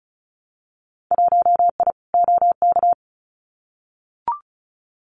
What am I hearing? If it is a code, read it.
Morse code, heard as a keyed tone, about 36 wpm, 700 Hz. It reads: 1I OK
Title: Winter Field Day Exchanges Encore 35wpm